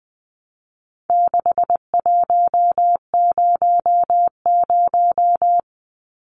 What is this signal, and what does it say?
Morse code, heard as a keyed tone, about 20 wpm, 700 Hz. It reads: 6100
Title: Numbers - 4-Digit 20wpm